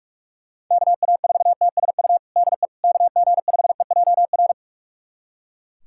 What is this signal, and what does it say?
Morse code, heard as a keyed tone, about 42 wpm, 700 Hz. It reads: KA4TSU DE KG5EJR
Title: USA Callsign DE Callsign - 6 Characters - 45wpm